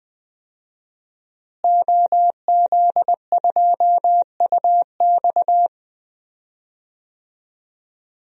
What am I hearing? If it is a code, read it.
Morse code, heard as a keyed tone, about 20 wpm, 700 Hz. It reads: OZ2UX
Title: DXCC-100 Call Signs - 5 Characters - Repeated 3x - 20wpm